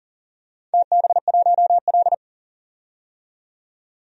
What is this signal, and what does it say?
Morse code, heard as a keyed tone, about 39 wpm, 700 Hz. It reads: TB1L
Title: DXCC-100 Call Signs - 4 Characters - 40wpm